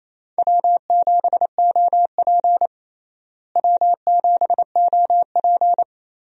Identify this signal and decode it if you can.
Morse code, heard as a keyed tone, about 28 wpm, 700 Hz. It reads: W7OP W7OP
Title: USA Call Signs - 4 Characters - Repeated 2x - 28wpm